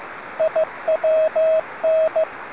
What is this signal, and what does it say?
Morse code, heard as a keyed tone, about 15 wpm, 630 Hz, with radio noise. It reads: IWN